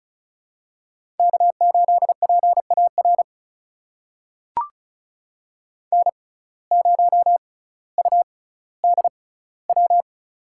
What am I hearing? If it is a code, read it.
Morse code, heard as a keyed tone, about 36 wpm, spaced as for 18 wpm, 700 Hz. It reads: K8PAR N0UDW
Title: USA Call Signs - 5 Characters - 6x Character Spacing - 35wpm